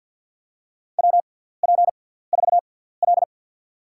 Morse code, heard as a keyed tone, about 53 wpm, spaced as for 17 wpm, 700 Hz. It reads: WP4L
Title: USA Call Signs - 4 Characters - 6x Character Spacing - 50wpm